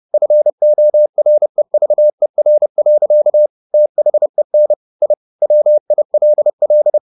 Morse code, heard as a keyed tone, about 30 wpm, 600 Hz. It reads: FOREVER. THEN I WILL
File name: the old man and the sea / the old man and the sea 1 30wpm